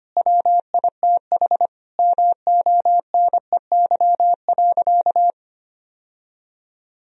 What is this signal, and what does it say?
Morse code, heard as a keyed tone, about 25 wpm, 700 Hz. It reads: WITH MONEY.
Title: Sentences from Top 1000 Words - Review Part 2 - 25wpm